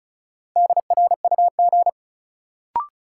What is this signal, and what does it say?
Morse code, heard as a keyed tone, about 36 wpm, 700 Hz. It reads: DRUG
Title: Sets of 5 Words - New Word Drill - 35wpm